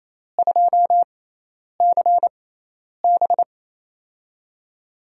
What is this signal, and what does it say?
Morse code, heard as a keyed tone, about 28 wpm, spaced as for 10 wpm, 700 Hz. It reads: 2CB
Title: USA Call Signs - 6x Character Spacing - 28wpm